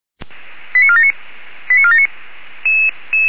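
Five-tone selective call (ZVEI), heard as quick runs of five tones, tones of about 70 ms. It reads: 89479 89479
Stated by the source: ZVEI five-tone selective call received on an SDR.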